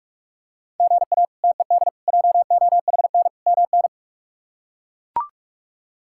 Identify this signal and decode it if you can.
Morse code, heard as a keyed tone, about 41 wpm, 700 Hz. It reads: GA TED JOHN MN
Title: K1USN SST QSO Practice - 45wpm